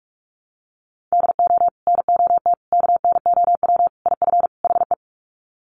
Morse code, heard as a keyed tone, about 44 wpm, 700 Hz. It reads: DO NOT KNOW IF HE
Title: Sets of 5 Words Encore2 45wpm